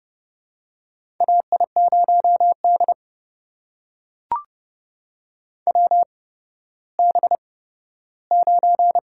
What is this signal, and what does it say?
Morse code, heard as a keyed tone, about 30 wpm, spaced as for 15 wpm, 700 Hz. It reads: AI0D WB9
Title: USA Call Signs - 4 Characters - 8x Character Spacing - 30wpm